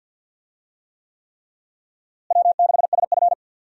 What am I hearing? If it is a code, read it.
Morse code, heard as a keyed tone, about 51 wpm, 700 Hz. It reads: W6SF
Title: USA Call Signs - 4 Characters - Repeated 3x - 50wpm